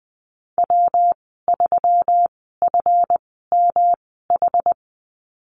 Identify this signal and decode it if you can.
Morse code, heard as a keyed tone, about 20 wpm, spaced as for 14 wpm, 700 Hz. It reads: W3FMH
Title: 245 Lesson 39g - Letter z - Call Signs 20wpm